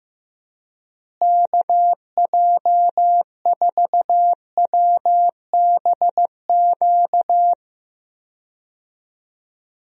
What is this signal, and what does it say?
Morse code, heard as a keyed tone, about 15 wpm, 700 Hz. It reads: KJ4WBQ
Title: USA Call Signs - 6 Characters - Repeated 2x - 15wpm